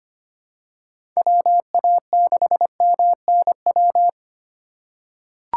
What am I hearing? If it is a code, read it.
Morse code, heard as a keyed tone, about 25 wpm, 700 Hz. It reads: WA6MNW
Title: USA Call Signs - 6 Characters - 2x Character Spacing - 25wpm